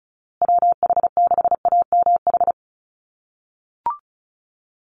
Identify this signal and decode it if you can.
Morse code, heard as a keyed tone, about 34 wpm, 700 Hz. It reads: WH6AMH
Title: USA Call Signs Repeated 3x 35wpm